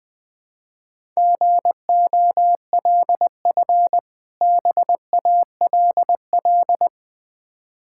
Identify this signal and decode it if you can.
Morse code, heard as a keyed tone, about 20 wpm, 700 Hz. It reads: GOLF BALL
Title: Difficult Words with Hints - 20wpm